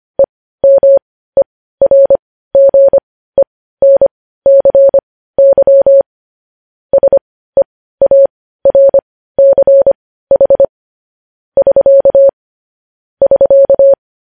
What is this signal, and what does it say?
Morse code, heard as a keyed tone, about 25 wpm, spaced as for 15 wpm, 550 Hz. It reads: EMERGENCY SEARCH <SK> <SK>
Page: News Headlines in Morse Code at 15 WPM